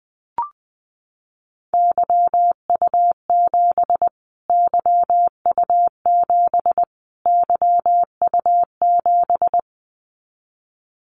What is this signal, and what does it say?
Morse code, heard as a keyed tone, about 20 wpm, 700 Hz. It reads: YU7 YU7 YU7
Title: Callsign Prefixes for Top 25 DXCC - Repeated 3x - 20wpm